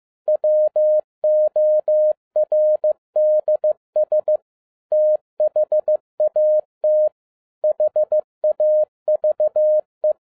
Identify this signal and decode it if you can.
Morse code, heard as a keyed tone, about 15 wpm, 610 Hz. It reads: WORDS THAT HAVE
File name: MY CW LIFE 1 15wpm